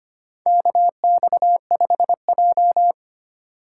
Morse code, heard as a keyed tone, about 25 wpm, 700 Hz. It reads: KX5J
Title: USA Call Signs - 4 Characters - 4x Character Spacing - 25wpm